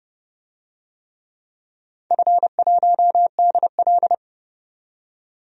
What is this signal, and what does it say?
Morse code, heard as a keyed tone, about 31 wpm, 700 Hz. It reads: F1DL
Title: DXCC-100 Call Signs - 4 Characters - 30wpm